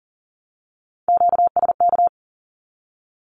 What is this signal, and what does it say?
Morse code, heard as a keyed tone, about 40 wpm, 700 Hz. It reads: QSK
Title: Q-Codes Repeated 3x 40wpm